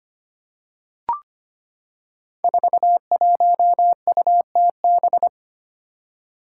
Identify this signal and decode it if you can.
Morse code, heard as a keyed tone, about 25 wpm, 700 Hz. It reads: 41UTB
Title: Five Challenging Letter-Number Combinations 25wpm